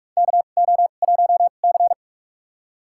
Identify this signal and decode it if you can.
Morse code, heard as a keyed tone, about 47 wpm, spaced as for 33 wpm, 700 Hz. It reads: KO1C
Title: USA Call Signs - 4 Characters - 2x Character Spacing - 45wpm